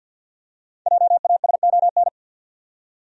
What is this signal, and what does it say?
Morse code, heard as a keyed tone, about 53 wpm, 700 Hz. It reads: JASON
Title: USA Common Names - 50wpm